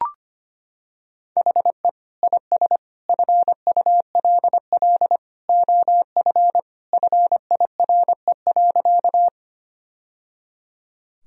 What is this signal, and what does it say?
Morse code, heard as a keyed tone, about 25 wpm, 700 Hz. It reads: HE IS FULL OF FIRE.
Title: Sentences from Top 1000 Words - Review Part 3 - 25wpm